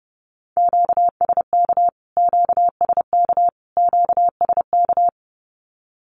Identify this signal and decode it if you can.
Morse code, heard as a keyed tone, about 30 wpm, 700 Hz. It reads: QSK QSK QSK
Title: Q-Codes Repeated 3x 30wpm